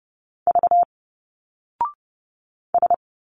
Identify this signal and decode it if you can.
Morse code, heard as a keyed tone, about 30 wpm, 700 Hz. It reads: V S
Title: Single Letters 30wpm